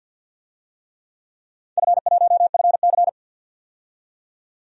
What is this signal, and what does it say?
Morse code, heard as a keyed tone, about 51 wpm, 700 Hz. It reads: F1FC
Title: DXCC-100 Call Signs - 4 Characters - 50wpm